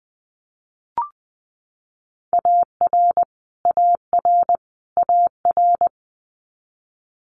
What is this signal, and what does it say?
Morse code, heard as a keyed tone, about 20 wpm, 700 Hz. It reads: AR AR AR
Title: Multi-letter Phonemes Repeated 3x 20wpm